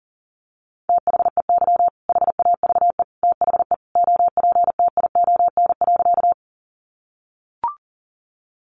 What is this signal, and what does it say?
Morse code, heard as a keyed tone, about 40 wpm, 700 Hz. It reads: THEY HAVE THE OPTION.
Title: Sentences from Top 1300 Words 40wpm